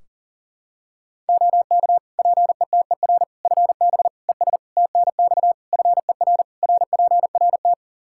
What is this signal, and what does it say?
Morse code, heard as a keyed tone, about 37 wpm, 700 Hz. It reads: OK PETER FB ES TNX FER RPRT
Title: K1USN SST QSO Practice 40wpm